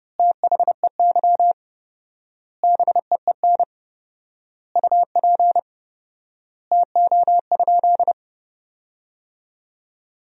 Morse code, heard as a keyed tone, about 32 wpm, 700 Hz. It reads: THEY BEEN UP TO?
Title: 4x Word-Spacing - Sentences from Top 100 Words - 30wpm